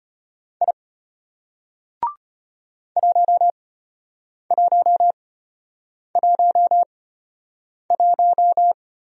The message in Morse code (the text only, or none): I 1111